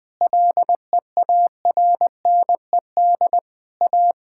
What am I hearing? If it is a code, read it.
Morse code, heard as a keyed tone, about 20 wpm, 700 Hz. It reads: LEARNED A
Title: Sentences with 5-Word Phrases - 20wpm